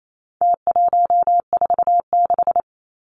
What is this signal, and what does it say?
Morse code, heard as a keyed tone, about 28 wpm, 700 Hz. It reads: T146
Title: Numbers - Contest Style - 4-Digit 28wpm